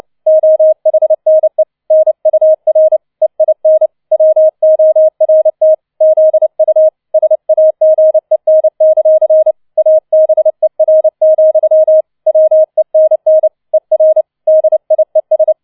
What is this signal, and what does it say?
Morse code, heard as a keyed tone, about 29 wpm, 630 Hz. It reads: OHNE NUR EIN WORT ZU SAGEN; ABER, WENN ER DIES